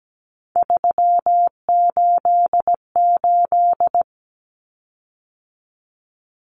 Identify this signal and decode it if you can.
Morse code, heard as a keyed tone, about 17 wpm, 700 Hz. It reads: 388
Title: Common Cut Numbers - 3-Digit 17wpm